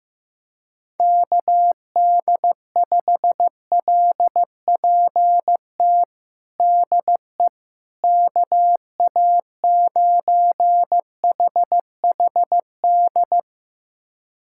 Morse code, heard as a keyed tone, about 15 wpm, 700 Hz. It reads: KD5LPT DE KA9HHD
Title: USA Callsign DE Callsign 15wpm